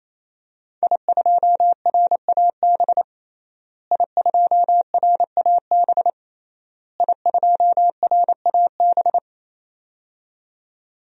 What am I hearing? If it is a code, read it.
Morse code, heard as a keyed tone, about 28 wpm, 700 Hz. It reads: I2RAB I2RAB I2RAB
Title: DXCC-100 Call Signs - 5 Characters - Repeated 3x - 28wpm